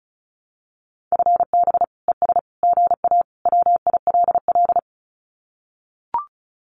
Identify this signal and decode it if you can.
Morse code, heard as a keyed tone, about 34 wpm, 700 Hz. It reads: FB ES GA WILL
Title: Common QSO Phrases Encore 35wpm